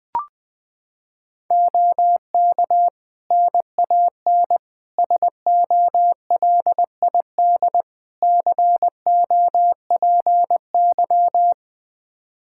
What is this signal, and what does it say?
Morse code, heard as a keyed tone, about 20 wpm, 700 Hz. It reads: OK NAN SOLID COPY
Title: Practice QSO - Exchange 3 - More Info - 20wpm